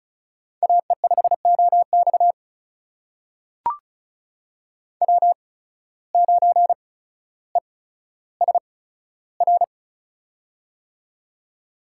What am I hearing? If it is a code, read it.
Morse code, heard as a keyed tone, about 36 wpm, spaced as for 15 wpm, 700 Hz. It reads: AE5OX W9ESR
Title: USA Call Signs - 5 Characters - 8x Character Spacing - 35wpm